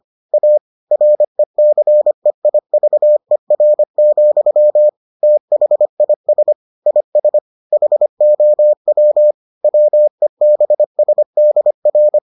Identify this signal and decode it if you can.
Morse code, heard as a keyed tone, about 25 wpm, 600 Hz. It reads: A RECEIVER, THIS IS HOW WEBSDR